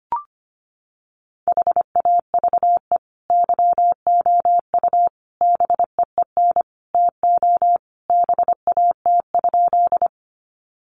Morse code, heard as a keyed tone, about 25 wpm, 700 Hz. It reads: HAVE YOU BEEN TO BAT?
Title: Sentences from Top 1500 Words 25wpm